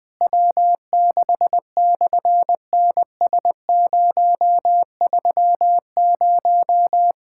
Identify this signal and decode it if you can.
Morse code, heard as a keyed tone, about 18 wpm, 700 Hz. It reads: W6/NS030
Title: SOTA QSO Practice 20wpm